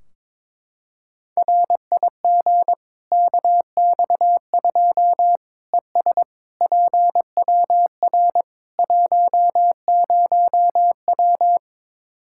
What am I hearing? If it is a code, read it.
Morse code, heard as a keyed tone, about 22 wpm, 700 Hz. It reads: RIG KX2 ES PWR 10W
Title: Practice QSO - Exchange 2 - Info - 22wpm